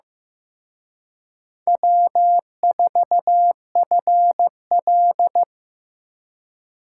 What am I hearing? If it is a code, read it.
Morse code, heard as a keyed tone, about 18 wpm, 700 Hz. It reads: W4FL
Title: Speed Racing - USA 1x2 Call Signs 15wpm